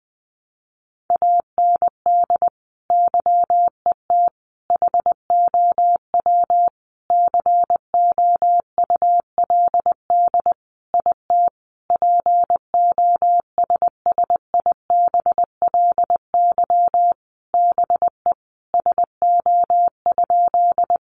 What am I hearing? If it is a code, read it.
Morse code, heard as a keyed tone, about 20 wpm, 700 Hz. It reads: AND YET HOW COULD IT POSSIBLY BE SO?
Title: Sentences from Top 1300 Words 20wpm